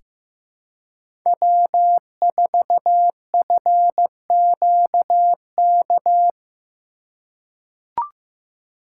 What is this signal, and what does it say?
Morse code, heard as a keyed tone, about 15 wpm, 700 Hz. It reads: W4FQK
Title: USA Call Signs - 5 Characters - 3x Character Spacing - 15wpm